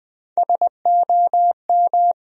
Morse code, heard as a keyed tone, about 20 wpm, 700 Hz. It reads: SOM
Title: USA Call Sign Postfixes - 3 Letters 20wpm